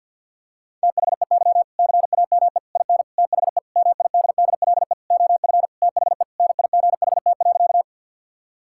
Morse code, heard as a keyed tone, about 50 wpm, 700 Hz. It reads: THEY CAME IN THE MIDDLE OF THE NIGHT.
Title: Sentences from Top 1000 Words - Review Part 6 - 50wpm